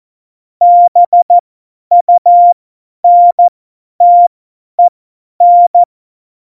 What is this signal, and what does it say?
Morse code, heard as a keyed tone, about 14 wpm, spaced as for 10 wpm, 700 Hz. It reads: BUNTEN
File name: Morsekurs / Witze / Witz 14 April 2020, 50 F70